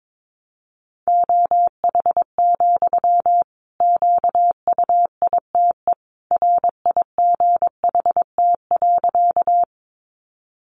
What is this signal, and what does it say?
Morse code, heard as a keyed tone, about 22 wpm, 700 Hz. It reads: OH, QUITE RIGHT.
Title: Sentences from Top 700 Words 22wpm